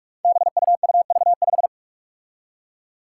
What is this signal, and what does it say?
Morse code, heard as a keyed tone, about 46 wpm, 700 Hz. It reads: BUUV5
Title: Five Challenging Letter-Number Combinations 45wpm